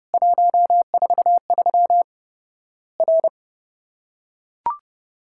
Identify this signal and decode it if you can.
Morse code, heard as a keyed tone, about 30 wpm, 700 Hz. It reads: 143 R
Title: Worked All Europe DX Contest - QTC Exchange - 30wpm